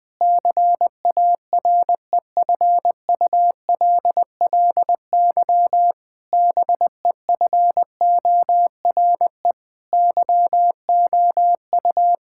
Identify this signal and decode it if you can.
Morse code, heard as a keyed tone, about 20 wpm, 700 Hz. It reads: CAREFULLY BEFORE YOU
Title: Sentences from Top 1600 Words 20wpm